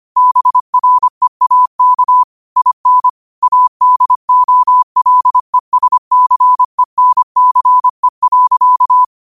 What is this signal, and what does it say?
Morse code, heard as a keyed tone, about 25 wpm, 1000 Hz. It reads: BREAK IN ADOLESCENCE.
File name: qotdfiles / 25wpm